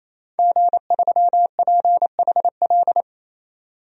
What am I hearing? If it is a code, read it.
Morse code, heard as a keyed tone, about 28 wpm, 700 Hz. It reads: G3PHL
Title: DXCC-100 Call Signs - 5 Characters - 28wpm